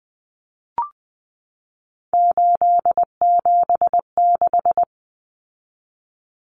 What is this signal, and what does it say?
Morse code, heard as a keyed tone, about 20 wpm, 700 Hz. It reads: 876
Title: Numbers - 3-Digit 20wpm